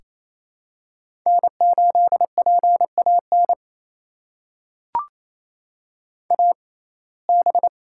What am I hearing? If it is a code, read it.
Morse code, heard as a keyed tone, about 28 wpm, spaced as for 20 wpm, 700 Hz. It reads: N8PAN AB
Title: USA Call Signs - 5 Characters - 6x Character Spacing - 28wpm